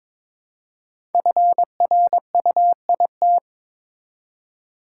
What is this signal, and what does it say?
Morse code, heard as a keyed tone, about 22 wpm, 700 Hz. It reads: FRUIT
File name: 5 Letter Words - First 3 Letters Given - 22wpm